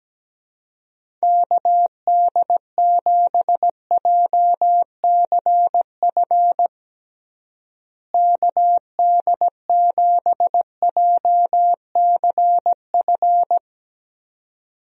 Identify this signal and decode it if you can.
Morse code, heard as a keyed tone, about 17 wpm, 700 Hz. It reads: KD7JCF KD7JCF
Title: USA Call Signs - 6 Characters - Repeated 2x - 17wpm